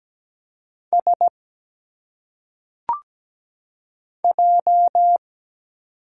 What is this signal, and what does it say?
Morse code, heard as a keyed tone, about 17 wpm, 700 Hz. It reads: S J
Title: Single Letters - Phonetic Pronunciation 17wpm